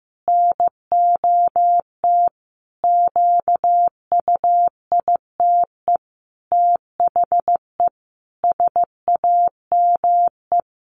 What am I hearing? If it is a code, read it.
Morse code, heard as a keyed tone, about 15 wpm, 700 Hz. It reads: NOT QUITE THE SAME
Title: A Sound Of Thunder 15wpm